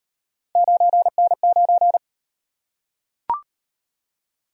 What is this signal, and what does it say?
Morse code, heard as a keyed tone, about 33 wpm, 700 Hz. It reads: 9N9